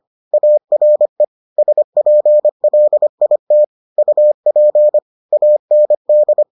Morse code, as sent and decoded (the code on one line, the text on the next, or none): .- .-. / ... .--. .-.. .. - / ..- .--. / .- -. -..
ARE SPLIT UP AND